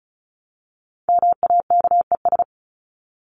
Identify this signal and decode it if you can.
Morse code, heard as a keyed tone, about 34 wpm, 700 Hz. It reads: MAKES
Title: ES:S Suffix - Top 100 Words with Base Words 35wpm